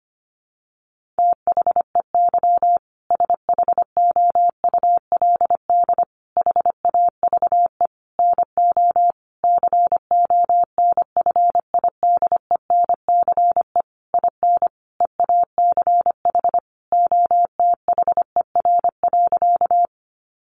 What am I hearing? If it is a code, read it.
Morse code, heard as a keyed tone, about 25 wpm, 700 Hz. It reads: THEY SHOULD HAVE NO CONFIDENCE IN EACH OTHER.
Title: Sentences from Top 1500 Words 25wpm